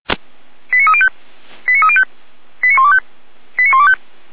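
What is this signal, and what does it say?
Five-tone selective call (ZVEI), heard as quick runs of five tones, tones of about 70 ms. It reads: 89305 89305 89125 89125
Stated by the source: ZVEI five-tone selective call received on an SDR, fire brigade Feuerwehr Orlamünde.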